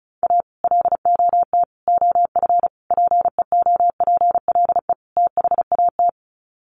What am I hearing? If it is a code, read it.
Morse code, heard as a keyed tone, about 34 wpm, 700 Hz. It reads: A LOT OF PEOPLE THAT
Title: Sets of 5 Words Encore2 35wpm